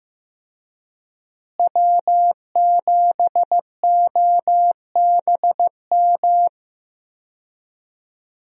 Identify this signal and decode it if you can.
Morse code, heard as a keyed tone, about 16 wpm, 700 Hz. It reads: W7OBM
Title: USA Call Signs - 5 Characters - Repeated 2x - 15wpm